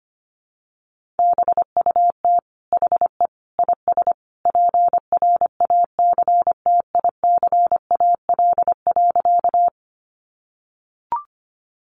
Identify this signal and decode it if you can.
Morse code, heard as a keyed tone, about 25 wpm, 700 Hz. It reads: BUT HE IS PRACTICAL.
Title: Sentences from Top 1400 Words 25wpm